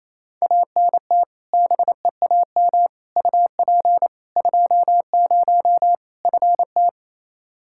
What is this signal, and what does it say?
Morse code, heard as a keyed tone, about 28 wpm, 700 Hz. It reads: ANT BEAM UP 20 FT
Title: K1USN SST QSO Practice 28wpm